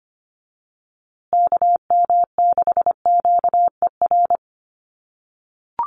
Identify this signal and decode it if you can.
Morse code, heard as a keyed tone, about 25 wpm, 700 Hz. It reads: KM6QER
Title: USA Call Signs Repeated 3x 25wpm